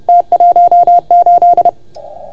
Morse code, heard as a keyed tone, about 30 wpm, 690 Hz. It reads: T18
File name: Archiv / CUC381 / OK1FGD CUC 381